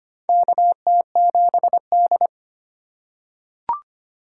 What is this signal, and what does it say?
Morse code, heard as a keyed tone, about 25 wpm, 700 Hz. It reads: KT7D
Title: USA Call Signs - 4x Character Spacing - 25wpm